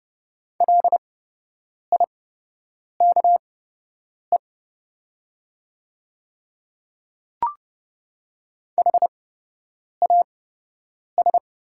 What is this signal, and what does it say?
Morse code, heard as a keyed tone, about 30 wpm, spaced as for 8 wpm, 700 Hz. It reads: LIKE HAS
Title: 8x Character Spacing - Top 100 Words - 30wpm